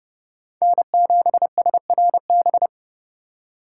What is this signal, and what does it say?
Morse code, heard as a keyed tone, about 38 wpm, 700 Hz. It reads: N7SRB